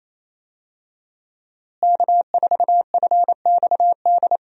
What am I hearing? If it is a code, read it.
Morse code, heard as a keyed tone, about 28 wpm, 700 Hz. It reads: K4FXD
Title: USA Call Signs - 5 Characters - Repeated 3x - 28wpm